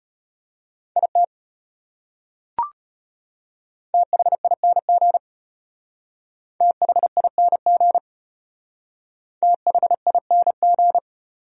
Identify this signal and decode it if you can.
Morse code, heard as a keyed tone, about 33 wpm, 700 Hz. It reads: IT THING THING THING